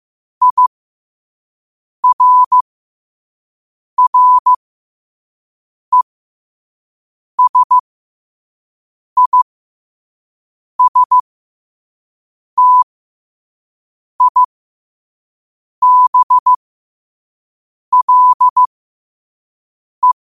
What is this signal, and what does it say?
Morse code, heard as a keyed tone, about 15 wpm, spaced as for 5 wpm, 1000 Hz. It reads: IRRESISTIBLE
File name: qotdfiles / 5wpm